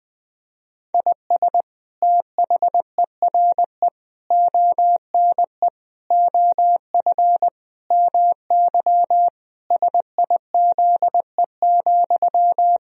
Morse code, heard as a keyed tone, about 20 wpm, 700 Hz. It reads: IS THERE ONE OF MY SIZE,
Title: Sentences from Top 1000 Words - Review Part 2 - 20wpm